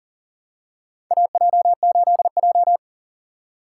Morse code, heard as a keyed tone, about 40 wpm, 700 Hz. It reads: AJ8J
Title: USA Call Signs - 4 Characters - 6x Character Spacing - 40wpm